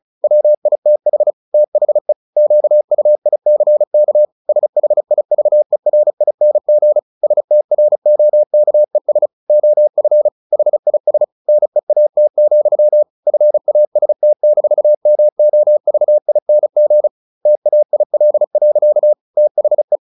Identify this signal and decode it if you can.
Morse code, heard as a keyed tone, about 34 wpm, 600 Hz. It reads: WITH THE QUICK SHIVERING STROKES OF HIS NEAT, FAST-MOVING TAIL. THE
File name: the old man and the sea / the old man and the sea 1 35wpm